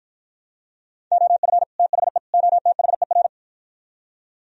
Morse code, heard as a keyed tone, about 57 wpm, 700 Hz. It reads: OF THE OTHER